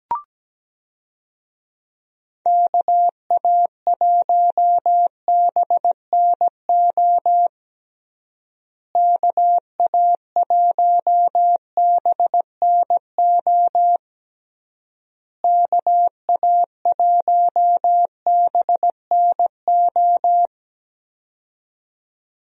Morse code, heard as a keyed tone, about 17 wpm, 700 Hz. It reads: KA1BNO KA1BNO KA1BNO
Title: USA Call Signs - 6 Characters - Repeated 3x - 17wpm